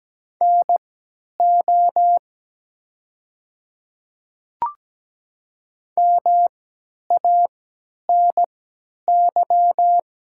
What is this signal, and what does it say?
Morse code, heard as a keyed tone, about 17 wpm, spaced as for 10 wpm, 700 Hz. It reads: NO MANY
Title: 3x Character Spacing - Top 100 Words - 17wpm